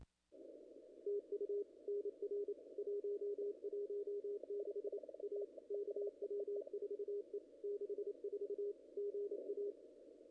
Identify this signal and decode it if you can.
Morse code, heard as a keyed tone, about 29 wpm, 440 Hz. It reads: TU NR 116 A KW4E 64 GA
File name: contests / ss-cw-k5tr-op-n5rz -0100